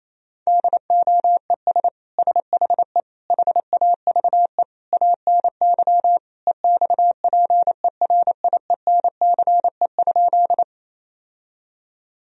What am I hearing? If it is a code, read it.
Morse code, heard as a keyed tone, about 28 wpm, 700 Hz. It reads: DOES SHE HAVE ANY EXPERIENCE?
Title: Sentences from Top 1000 Words - Review Part 3 - 28wpm